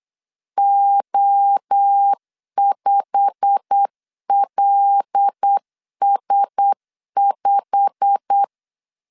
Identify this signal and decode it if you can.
Morse code, heard as a keyed tone, about 8 wpm, 800 Hz. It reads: O5LS5
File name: LEZIONI / 8 YJ45 / 8 YJ45 8wpm 3214